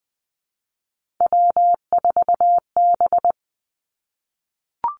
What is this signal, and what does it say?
Morse code, heard as a keyed tone, about 20 wpm, 700 Hz. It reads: W4B
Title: Three Challenging Letter-Number Combinations 20wpm